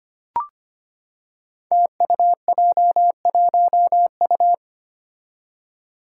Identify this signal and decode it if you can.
Morse code, heard as a keyed tone, about 25 wpm, 700 Hz. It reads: TUJ1U
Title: Five Challenging Letter-Number Combinations 25wpm